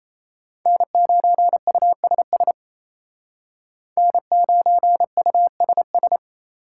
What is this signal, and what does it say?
Morse code, heard as a keyed tone, about 34 wpm, 700 Hz. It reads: N9USS N9USS
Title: Speed Racing - USA Call Signs 28wpm